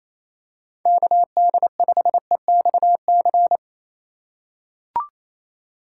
Morse code, heard as a keyed tone, about 28 wpm, 700 Hz. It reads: KD5EXC
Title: USA Call Signs - 3x Character Spacing - 28wpm